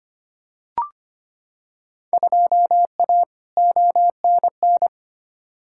Morse code, heard as a keyed tone, about 25 wpm, 700 Hz. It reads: 2A ONN
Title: Field Day Exchanges Encore - 25wpm